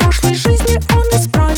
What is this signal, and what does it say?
There is no code.